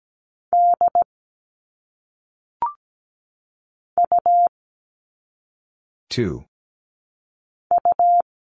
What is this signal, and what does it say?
Morse code, heard as a keyed tone, about 17 wpm, 700 Hz. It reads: D U U
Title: All Cut Numbers - 1-Digit 17wpm